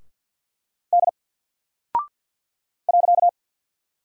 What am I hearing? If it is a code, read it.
Morse code, heard as a keyed tone, about 48 wpm, 700 Hz. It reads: D .
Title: Single Character (Letter:Number:Punctuation:Prosign) Alt 50wpm